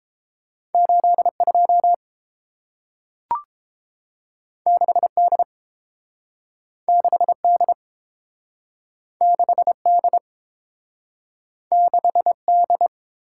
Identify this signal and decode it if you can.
Morse code, heard as a keyed tone, about 30 wpm, 700 Hz. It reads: 82 6D 6D 6D 6D